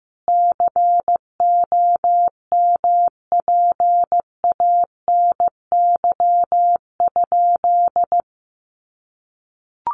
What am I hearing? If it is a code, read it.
Morse code, heard as a keyed tone, about 15 wpm, 700 Hz. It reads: COMPANY?
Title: Sentences from Top 1000 Words 15wpm